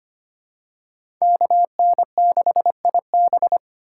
Morse code, heard as a keyed tone, about 25 wpm, 700 Hz. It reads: KN6IB
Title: USA Call Signs - 5 Characters - 3x Character Spacing - 25wpm